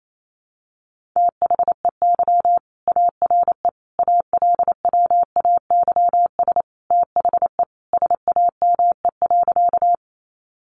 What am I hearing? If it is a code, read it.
Morse code, heard as a keyed tone, about 28 wpm, 700 Hz. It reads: THEY ARE ALWAYS THE SAME.
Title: Sentences from Top 300 Words 28wpm